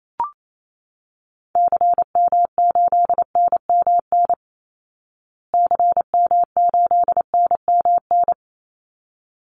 Morse code, heard as a keyed tone, about 28 wpm, 700 Hz. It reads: CM8NMN CM8NMN
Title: Call Signs Repeated 3x - 28wpm